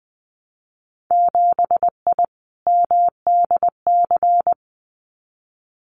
Morse code, heard as a keyed tone, about 20 wpm, 700 Hz. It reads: 7I MDC
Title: Winter Field Day Exchanges Encore 20wpm